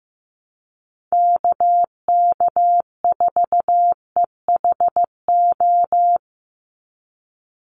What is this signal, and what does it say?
Morse code, heard as a keyed tone, about 15 wpm, 700 Hz. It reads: KK4EHO
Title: USA Call Signs 15wpm